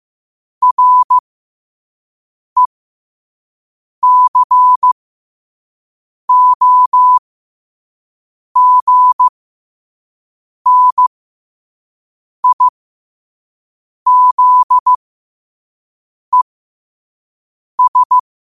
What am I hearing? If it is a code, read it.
Morse code, heard as a keyed tone, about 15 wpm, spaced as for 5 wpm, 1000 Hz. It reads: RECOGNIZES
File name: qotdfiles / 5wpm